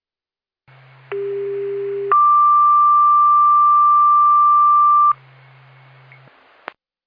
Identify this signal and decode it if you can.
Two-tone page, heard as a one-second tone, then a three-second tone, about 400 then 1180 Hz. It is heard from 1.1 to 5.1 s.